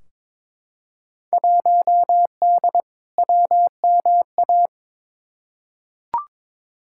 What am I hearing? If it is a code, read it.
Morse code, heard as a keyed tone, about 22 wpm, 700 Hz. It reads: 1D WMA
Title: Field Day Exchanges - 22wpm